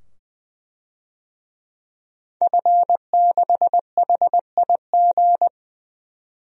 Morse code, heard as a keyed tone, about 21 wpm, 700 Hz. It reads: F6HIG
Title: DXCC-100 Call Signs - 5 Characters - 20wpm